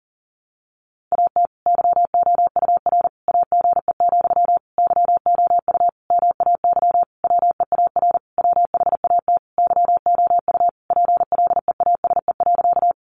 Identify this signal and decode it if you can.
Morse code, heard as a keyed tone, about 40 wpm, 700 Hz. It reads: AT YOUR AGE, YOU MAY WEAR WHAT YOU PLEASE.
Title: Sentences from Top 1500 Words 40wpm